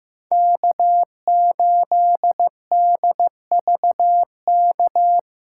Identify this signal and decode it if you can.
Morse code, heard as a keyed tone, about 15 wpm, 700 Hz. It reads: K8DVK
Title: USA Callsign DE Callsign 15wpm